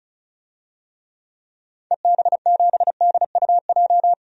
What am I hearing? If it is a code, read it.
Morse code, heard as a keyed tone, about 34 wpm, 700 Hz. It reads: EB7DUJ
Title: DXCC-100 Call Signs - 6 Characters - 35wpm